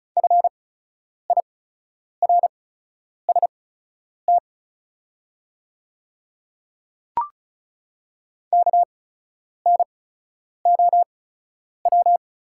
Morse code, heard as a keyed tone, about 36 wpm, spaced as for 10 wpm, 700 Hz. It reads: FIRST KNOW
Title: 8x Character Spacing - Top 100 Words - 35wpm